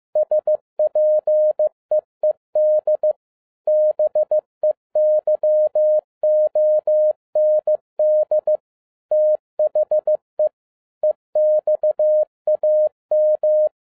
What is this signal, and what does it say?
Morse code, heard as a keyed tone, about 15 wpm, 610 Hz. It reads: SPEED BEYOND THE EXAM
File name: MY CW LIFE 1 15wpm